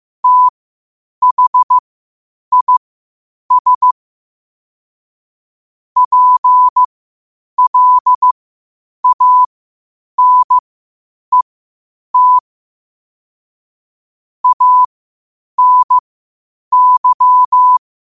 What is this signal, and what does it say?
Morse code, heard as a keyed tone, about 15 wpm, spaced as for 8 wpm, 1000 Hz. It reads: THIS PLANET ANY
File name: qotdfiles / 8wpm